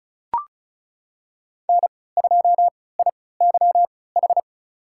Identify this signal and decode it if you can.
Morse code, heard as a keyed tone, about 34 wpm, spaced as for 20 wpm, 700 Hz. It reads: N2IYH
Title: USA Call Signs - 3x Character Spacing - 35wpm